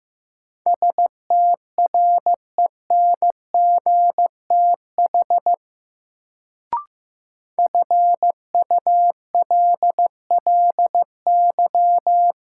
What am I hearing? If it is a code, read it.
Morse code, heard as a keyed tone, about 15 wpm, 700 Hz. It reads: STRENGTH FULLY
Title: Sentences from Top 1300 Words - New Word Drill Alt - 15wpm